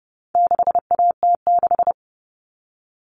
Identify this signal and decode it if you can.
Morse code, heard as a keyed tone, about 30 wpm, 700 Hz. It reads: 6AT6
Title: All Cut Numbers - 4-Digit 30wpm